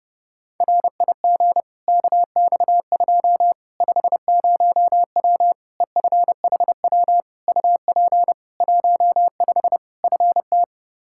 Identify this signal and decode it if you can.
Morse code, heard as a keyed tone, about 30 wpm, 700 Hz. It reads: RIG KX2 50W EFHW UP 15 FT
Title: K1USN SST QSO Practice 30wpm